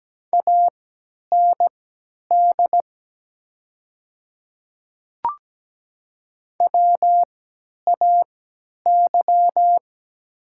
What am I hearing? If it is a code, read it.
Morse code, heard as a keyed tone, about 17 wpm, spaced as for 10 wpm, 700 Hz. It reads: AND WAY
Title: 3x Character Spacing - Top 100 Words - 17wpm